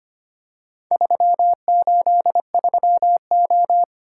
Morse code, heard as a keyed tone, about 25 wpm, 700 Hz. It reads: 383O
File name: practice / 25wpm / Four Letter-Number Combinations - 25wpm